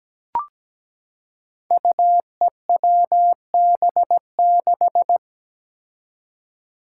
Morse code, heard as a keyed tone, about 17 wpm, 700 Hz. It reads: UEWB6
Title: Five Challenging Letter-Number Combinations 17wpm